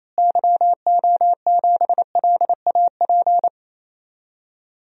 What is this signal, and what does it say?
Morse code, heard as a keyed tone, about 28 wpm, 700 Hz. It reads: YO7LAP
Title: DXCC-100 Call Signs - 6 Characters - Repeated 3x - 28wpm